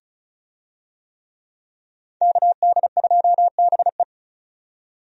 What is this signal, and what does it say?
Morse code, heard as a keyed tone, about 36 wpm, 700 Hz. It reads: KD2BE
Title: USA Call Signs - 5 Characters - 35wpm